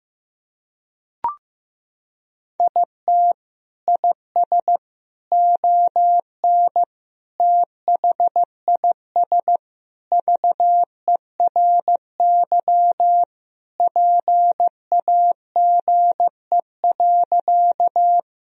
Morse code, heard as a keyed tone, about 15 wpm, 700 Hz. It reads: IT IS ON THIS VERY PAGE.
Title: Sentences from Top 1000 Words - Review Part 1 - 15wpm